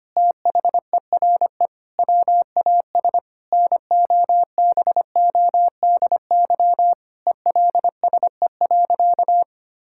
Morse code, heard as a keyed tone, about 25 wpm, 700 Hz. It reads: THERE WAS NOBODY ELSE.
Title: Sentences from Top 1600 Words 25wpm